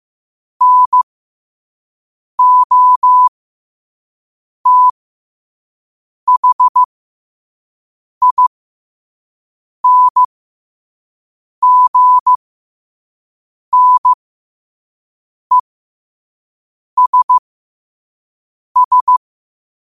Morse code, heard as a keyed tone, about 15 wpm, spaced as for 5 wpm, 1000 Hz. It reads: NOTHINGNESS
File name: qotdfiles / 5wpm